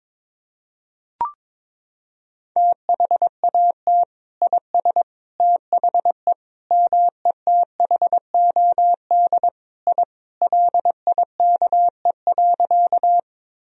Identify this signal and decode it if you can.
Morse code, heard as a keyed tone, about 22 wpm, 700 Hz. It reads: THAT IS THE METHOD I LIKE.
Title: Sentences from Top 1000 Words - Review Part 3 - 22wpm